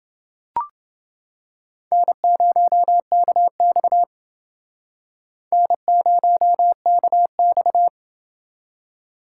Morse code, heard as a keyed tone, about 28 wpm, 700 Hz. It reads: N0KX N0KX